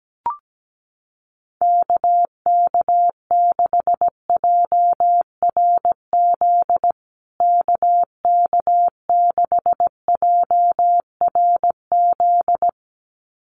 Morse code, heard as a keyed tone, about 17 wpm, 700 Hz. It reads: KK6JRZ KK6JRZ
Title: USA Call Signs Repeated 2x 17wpm